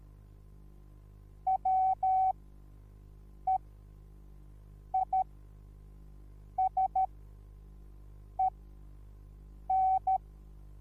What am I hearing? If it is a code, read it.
Morse code, heard as a keyed tone, about 13 wpm, spaced as for 5 wpm, 760 Hz. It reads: WEISEN